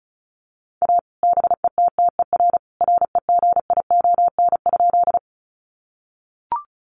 Morse code, heard as a keyed tone, about 34 wpm, 700 Hz. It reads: A BETTER REGION?
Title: Sentences from Top 1500 Words 35wpm